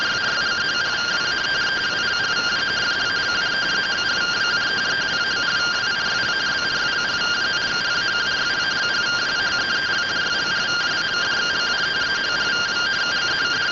Two-tone teleprinter signal, about 170 Hz shift.